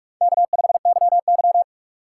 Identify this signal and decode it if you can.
Morse code, heard as a keyed tone, about 46 wpm, 700 Hz. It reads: K5YY
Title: USA Call Signs - 3x Character Spacing - 45wpm